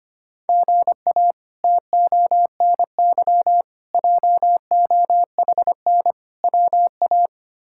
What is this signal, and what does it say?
Morse code, heard as a keyed tone, about 25 wpm, 700 Hz. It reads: GA TONY JOHN WA
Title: K1USN SST QSO Practice - 25wpm